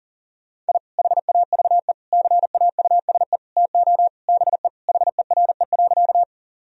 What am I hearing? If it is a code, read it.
Morse code, heard as a keyed tone, about 40 wpm, 700 Hz. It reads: I HAVE CAUSE TO BE HERE.
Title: Sentences from Top 1000 Words - Review Part 1 - 40wpm